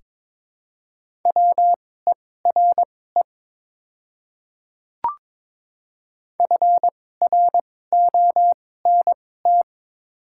Morse code, heard as a keyed tone, about 23 wpm, spaced as for 16 wpm, 700 Hz. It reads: WERE FRONT
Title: 2x Character Spacing - Top 500 Words - 22wpm